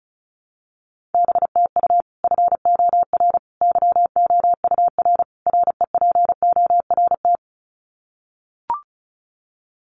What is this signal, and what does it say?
Morse code, heard as a keyed tone, about 34 wpm, 700 Hz. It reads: BTU FOR YOUR REPORT
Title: Common QSO Phrases 35wpm